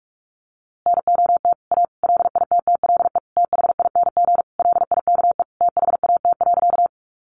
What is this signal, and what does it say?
Morse code, heard as a keyed tone, about 44 wpm, 700 Hz. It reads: NOT A LITTLE THING LIKE THAT.
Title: A Sound Of Thunder 45wpm